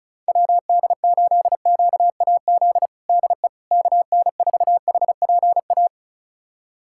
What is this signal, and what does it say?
Morse code, heard as a keyed tone, about 34 wpm, 700 Hz. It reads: WD8QAZ DE KN4HPA
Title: USA Callsign DE Callsign - 6 Characters - 35wpm